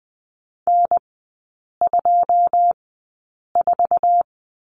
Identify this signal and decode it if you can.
Morse code, heard as a keyed tone, about 20 wpm, spaced as for 8 wpm, 690 Hz. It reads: N24
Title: 081 Lesson 14e - Number 2 - Three Characters 20wpm